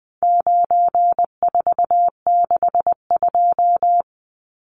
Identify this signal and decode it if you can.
Morse code, heard as a keyed tone, about 19 wpm, 700 Hz. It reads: 9462
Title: Numbers - 4-Digit 20wpm